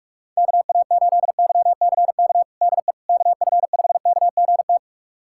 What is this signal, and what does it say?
Morse code, heard as a keyed tone, about 46 wpm, 700 Hz. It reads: KA8YCK DE KF5KGT
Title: USA Callsign DE Callsign - 6 Characters - 45wpm